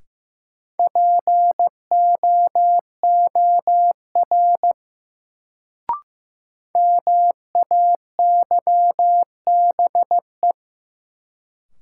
Morse code, heard as a keyed tone, about 15 wpm, 700 Hz. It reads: POOR MAYBE
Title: Sentences from Top 1000 Words - New Word Drill - 15wpm